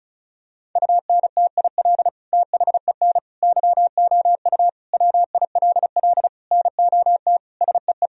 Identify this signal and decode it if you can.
Morse code, heard as a keyed tone, about 36 wpm, 700 Hz. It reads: UNTIL THEN YOU WILL NOT SEE
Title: Sentences from Top 1000 Words - Review Part 1 - 35wpm